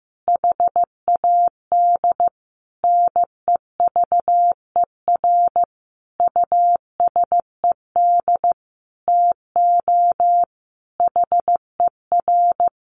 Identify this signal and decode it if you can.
Morse code, heard as a keyed tone, about 15 wpm, 700 Hz. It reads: HAD NEVER USED TO HER
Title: Sentences from Top 600 Words 15wpm